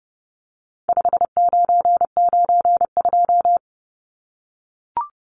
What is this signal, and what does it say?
Morse code, heard as a keyed tone, about 30 wpm, 700 Hz. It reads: 5992
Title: Numbers - 4-Digit 30wpm